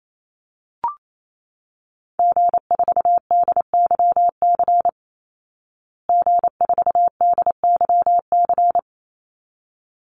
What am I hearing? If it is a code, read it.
Morse code, heard as a keyed tone, about 28 wpm, 700 Hz. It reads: G4DYC G4DYC
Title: Call Signs Repeated 3x - 28wpm